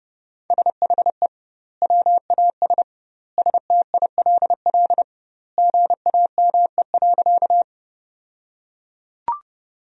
Morse code, heard as a keyed tone, about 30 wpm, 700 Hz. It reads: SHE WAS STILL GAME.
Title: 2x Word-Spacing - Sentences from Top 500 Words - 30wpm